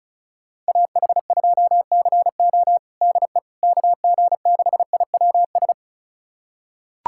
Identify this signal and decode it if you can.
Morse code, heard as a keyed tone, about 36 wpm, 700 Hz. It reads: AH2CO DE KG6IWS
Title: USA Callsign DE Callsign 35wpm